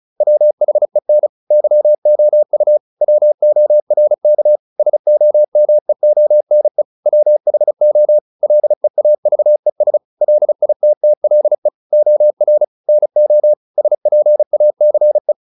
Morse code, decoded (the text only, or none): WHEN YOU WORK SOMEONE WHO LEAVES LITTLE OR NO SPACE